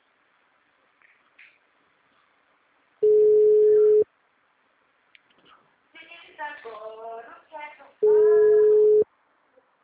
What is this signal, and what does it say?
Ringback tone (Europe, 425 Hz), with speech.